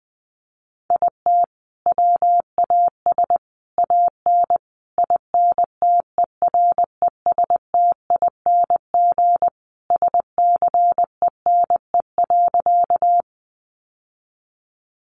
Morse code, heard as a keyed tone, about 19 wpm, 700 Hz. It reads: IT WAS AN INTERESTING SCENE.
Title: Sentences from Top 1100 Words 20wpm